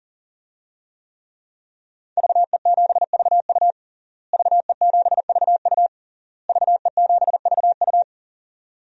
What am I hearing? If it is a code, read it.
Morse code, heard as a keyed tone, about 40 wpm, 700 Hz. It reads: VE7VU VE7VU VE7VU
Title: DXCC-100 Call Signs - 5 Characters - Repeated 3x - 40wpm